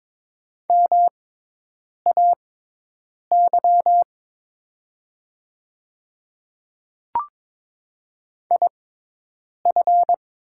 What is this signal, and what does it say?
Morse code, heard as a keyed tone, about 22 wpm, spaced as for 8 wpm, 700 Hz. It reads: MAY IF
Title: 6x Character Spacing - Top 100 Words - 22wpm